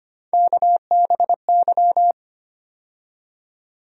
Morse code, heard as a keyed tone, about 25 wpm, 700 Hz. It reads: KBY
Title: USA Call Sign Postfixes - 3 Letters 25wpm